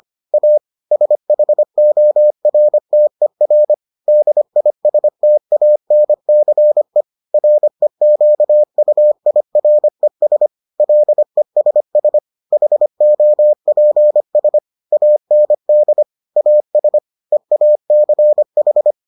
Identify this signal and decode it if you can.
Morse code, heard as a keyed tone, about 25 wpm, 600 Hz. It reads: A SHORTER DISTANCE REQUIRES LESS HOPS AND AS EACH